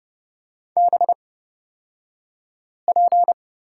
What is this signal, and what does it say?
Morse code, heard as a keyed tone, about 30 wpm, spaced as for 5 wpm, 700 Hz. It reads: BP
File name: Lesson 36d - BT - Two Characters 30wpm